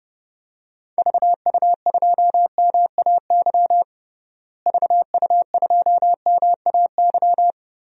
Morse code, heard as a keyed tone, about 31 wpm, 700 Hz. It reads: VU2MAY VU2MAY
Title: DXCC-100 Call Signs - 6 Characters - Repeated 2x - 30wpm